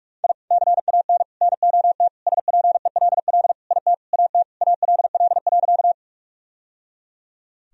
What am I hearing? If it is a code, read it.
Morse code, heard as a keyed tone, about 46 wpm, 700 Hz. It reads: I CAN NOT SPELL IT AT ALL.
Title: Sentences from Top 1000 Words - Review Part 1 - 45wpm